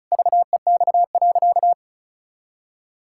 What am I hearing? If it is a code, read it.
Morse code, heard as a keyed tone, about 36 wpm, 700 Hz. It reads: VEX.
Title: Pangram Challenge v2 - 35wpm